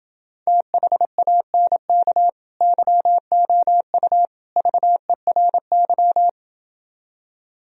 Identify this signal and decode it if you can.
Morse code, heard as a keyed tone, about 29 wpm, 700 Hz. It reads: THANK YOU VERY